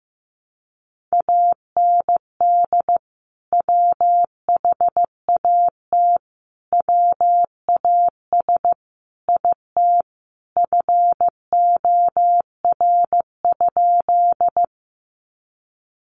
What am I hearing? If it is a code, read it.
Morse code, heard as a keyed tone, about 15 wpm, 700 Hz. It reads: AND WHAT WAS IT FOR?
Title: Sentences from Top 100 Words 15wpm